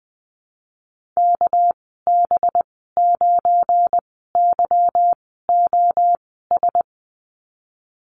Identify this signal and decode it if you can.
Morse code, heard as a keyed tone, about 19 wpm, spaced as for 14 wpm, 700 Hz. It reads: KB9YOS
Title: 219 Lesson 35g - Number 0 - Call Signs 20wpm